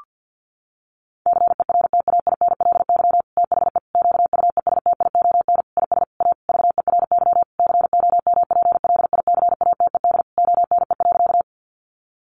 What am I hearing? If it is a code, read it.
Morse code, heard as a keyed tone, about 50 wpm, 700 Hz. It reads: CERTAINLY THE QUESTION IS A VERY COMPLICATED ONE.
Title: Sentences from Top 1200 Words 50wpm